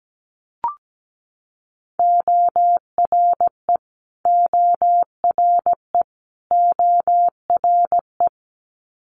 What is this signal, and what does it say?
Morse code, heard as a keyed tone, about 17 wpm, 700 Hz. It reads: ORE ORE ORE
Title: Multi-letter Phonemes Repeated 3x 17wpm